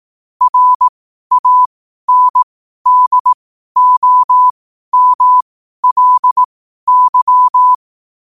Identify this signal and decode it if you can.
Morse code, heard as a keyed tone, about 18 wpm, spaced as for 12 wpm, 1000 Hz. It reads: RANDOMLY
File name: qotdfiles / 12wpm